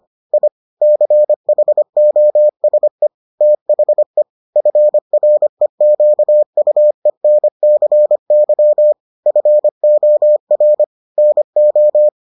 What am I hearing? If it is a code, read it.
Morse code, heard as a keyed tone, about 25 wpm, 600 Hz. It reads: I CHOSE THE FREQUENCY FOR NO